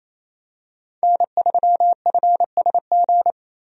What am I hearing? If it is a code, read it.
Morse code, heard as a keyed tone, about 27 wpm, 700 Hz. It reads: N3FSG
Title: USA Call Signs - 5 Characters - 28wpm